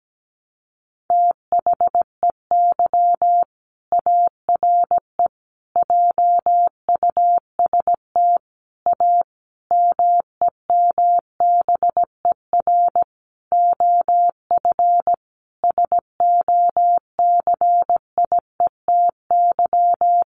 Morse code, heard as a keyed tone, about 17 wpm, 700 Hz. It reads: THEY ARE JUST A MEMBER OF SOCIETY
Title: Sentences from Top 1100 Words 17wpm